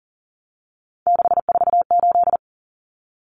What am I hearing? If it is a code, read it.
Morse code, heard as a keyed tone, about 40 wpm, 700 Hz. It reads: 648
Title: Numbers - 3-Digit 40wpm